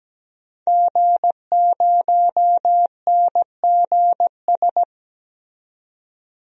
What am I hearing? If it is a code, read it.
Morse code, heard as a keyed tone, about 17 wpm, 700 Hz. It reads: G0NGS
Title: DXCC-100 Call Signs - 5 Characters - 17wpm